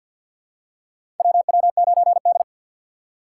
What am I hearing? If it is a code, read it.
Morse code, heard as a keyed tone, about 53 wpm, 700 Hz. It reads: WW9D
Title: USA Call Signs - 4 Characters - 3x Character Spacing - 50wpm